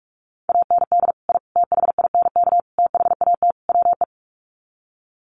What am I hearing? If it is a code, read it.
Morse code, heard as a keyed tone, about 44 wpm, 690 Hz. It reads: AND I THINK THAT WE
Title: Sets of 5 Words Encore2 45wpm